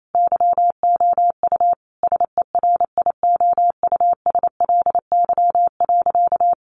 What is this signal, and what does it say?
Morse code, heard as a keyed tone, about 28 wpm, 700 Hz. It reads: YOU SERIOUSLY.
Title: Sentences from Top 1200 Words 28wpm